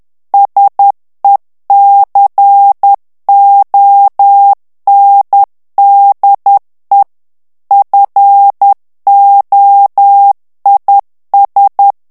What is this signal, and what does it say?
Morse code, heard as a keyed tone, about 10 wpm, 800 Hz. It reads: SECONDE FOIS
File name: morse mp3 / FRANCAIS TEXTE CLAIR 2 VITESSE 10 MOTS MINUTES